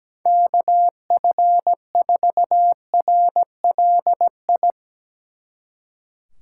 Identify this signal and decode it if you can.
Morse code, heard as a keyed tone, about 17 wpm, 700 Hz. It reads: KF4RLI
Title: Speed Racing - USA Call Signs 17wpm